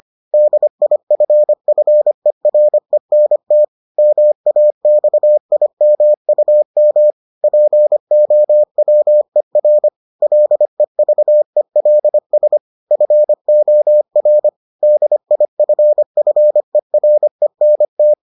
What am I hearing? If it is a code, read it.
Morse code, heard as a keyed tone, about 25 wpm, 600 Hz. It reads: DIFFERENT MAXIMUM POWER LEVELS FOR DIFFERENT